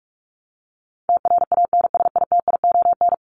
Morse code, heard as a keyed tone, about 44 wpm, 700 Hz. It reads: TRANSITION
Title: Top 1751-2000 Words 45wpm